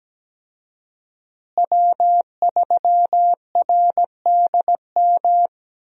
Morse code, heard as a keyed tone, about 17 wpm, 700 Hz. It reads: W3RDM
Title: USA Call Signs - 5 Characters - Repeated 2x - 17wpm